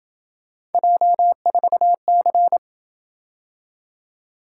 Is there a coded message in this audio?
Yes: Morse code from 0.7 to 2.6 s, heard as a keyed tone, about 28 wpm, 700 Hz. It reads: J4C